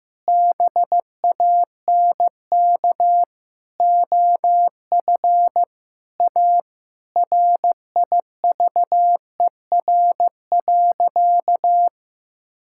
Morse code, heard as a keyed tone, about 15 wpm, 700 Hz. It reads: BANK OF A RIVER.
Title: Sentences from Top 1000 Words - Review Part 6 - 15wpm